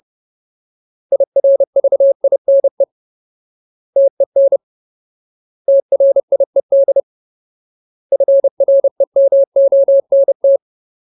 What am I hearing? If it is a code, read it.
Morse code, heard as a keyed tone, about 30 wpm, 550 Hz. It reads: IRVINE TEN TRIED FREMONT
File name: Ultimate Word List / 30WPM Ultimate